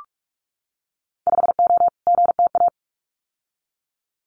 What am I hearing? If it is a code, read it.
Morse code, heard as a keyed tone, about 46 wpm, 700 Hz. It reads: 5O GTA
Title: Winter Field Day Exchanges Encore 45wpm